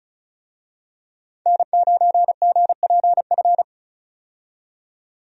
Morse code, heard as a keyed tone, about 36 wpm, 700 Hz. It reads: N9GPF
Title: USA Call Signs - 5 Characters - 35wpm